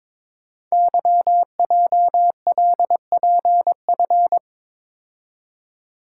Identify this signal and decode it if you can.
Morse code, heard as a keyed tone, about 22 wpm, 700 Hz. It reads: YJLPF
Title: Code Groups - 5 Letters - 22wpm